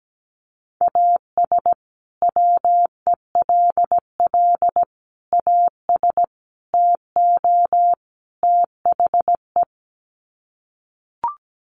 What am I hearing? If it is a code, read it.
Morse code, heard as a keyed tone, about 17 wpm, 700 Hz. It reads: AS WELL AS TO THE
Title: Sets of 5 Words Encore2 17wpm